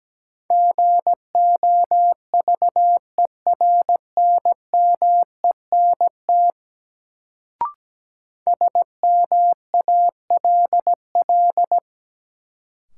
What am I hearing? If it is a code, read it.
Morse code, heard as a keyed tone, about 17 wpm, 700 Hz. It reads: GOVERNMENT SMALL
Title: Sets of 2 Words - New Word Drill - 17wpm